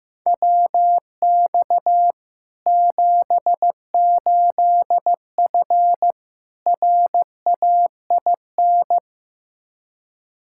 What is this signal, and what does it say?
Morse code, heard as a keyed tone, about 15 wpm, 700 Hz. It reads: WX 78F RAIN
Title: K1USN SST QSO Practice 15wpm